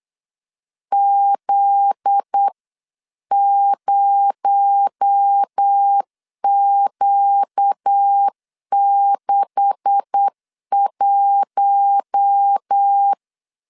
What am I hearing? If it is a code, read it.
Morse code, heard as a keyed tone, about 8 wpm, 800 Hz. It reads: Z0Q61
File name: CW / DET / 1 9 1500C S B 8wpm 5624